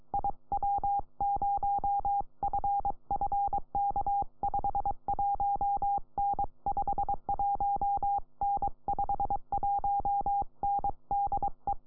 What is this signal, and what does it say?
Morse code, heard as a keyed tone, about 22 wpm, 800 Hz. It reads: IW0FFK 5 1N 5 1N 5 1N DE
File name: iw0ffk / 2007 05 09 PJ4 PA3CNX IW0FFK